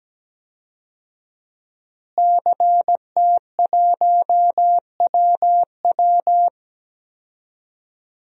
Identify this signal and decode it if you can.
Morse code, heard as a keyed tone, about 17 wpm, 700 Hz. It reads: CT1WW
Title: DXCC-100 Call Signs - 5 Characters - Repeated 3x - 17wpm